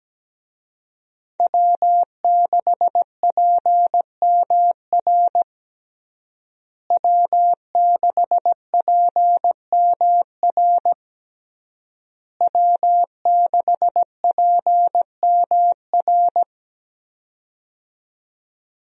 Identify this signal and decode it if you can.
Morse code, heard as a keyed tone, about 17 wpm, 700 Hz. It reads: W6PMR W6PMR W6PMR
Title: USA Call Signs - 5 Characters - Repeated 3x - 17wpm